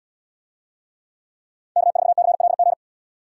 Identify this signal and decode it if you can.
Morse code, heard as a keyed tone, about 78 wpm, 700 Hz. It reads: K4CLG